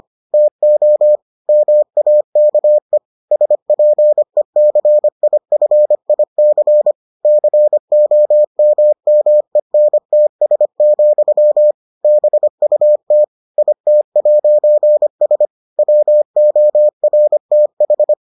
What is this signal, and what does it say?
Morse code, heard as a keyed tone, about 25 wpm, 600 Hz. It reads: TO MAKE SPECIFIC COMMENTS, BUT IT'S WORTH